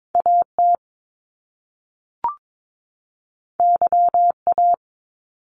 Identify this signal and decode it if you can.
Morse code, heard as a keyed tone, about 22 wpm, 700 Hz. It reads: AT YA
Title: Top Two Letter Combinations in Words 22wpm